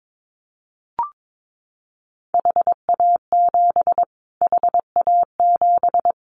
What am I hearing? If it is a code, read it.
Morse code, heard as a keyed tone, about 22 wpm, 700 Hz. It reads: HA7 HA7
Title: Callsign Prefixes for Top 25 DXCC - Repeated 3x - 22wpm